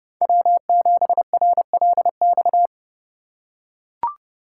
Morse code, heard as a keyed tone, about 29 wpm, 700 Hz. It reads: W7RLX